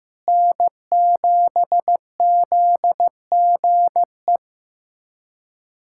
Morse code, heard as a keyed tone, about 15 wpm, 700 Hz. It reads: N7ZGE
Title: USA Callsign DE Callsign - 5 Characters - 15wpm